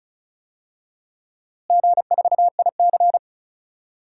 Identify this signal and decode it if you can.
Morse code, heard as a keyed tone, about 36 wpm, 700 Hz. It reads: G4IC
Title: DXCC-100 Call Signs - 4 Characters - Repeated 3x - 35wpm